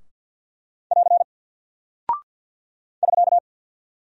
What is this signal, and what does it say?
Morse code, heard as a keyed tone, about 50 wpm, 700 Hz. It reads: <AR> <SK>
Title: Single Character (Letter:Number:Punctuation:Prosign) Alt 50wpm